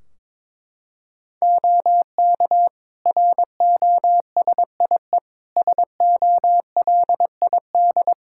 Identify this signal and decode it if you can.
Morse code, heard as a keyed tone, about 21 wpm, 700 Hz. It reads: OK ROSIE SOLID
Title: Practice QSO - Exchange 3 - More Info - 22wpm